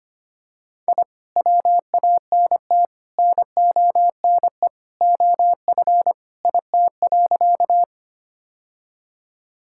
Morse code, heard as a keyed tone, about 25 wpm, 690 Hz. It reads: I WANT NONE OF IT.
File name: practice / 25wpm / Sentences with 2-Word Phrases - 25wpm